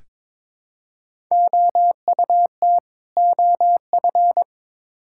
Morse code, heard as a keyed tone, about 21 wpm, 700 Hz. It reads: OUT OF
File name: practice / 22wpm / Build A Phrase - 5-Word Phrases - 22wpm